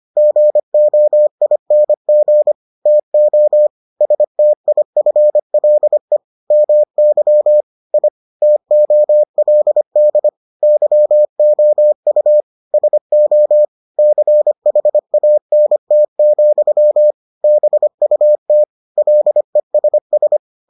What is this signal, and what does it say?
Morse code, heard as a keyed tone, about 25 wpm, 600 Hz. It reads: GOING TO STIFLE MY I TOLD YOU SO CHANT, BUT LESS